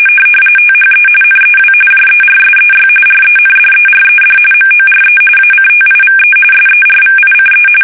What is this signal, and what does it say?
RTTY, about 45.45 baud. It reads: THE NEW JERSEY RENDERED MORE ACTIVE SERVICE
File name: fsk850 / battleshipnj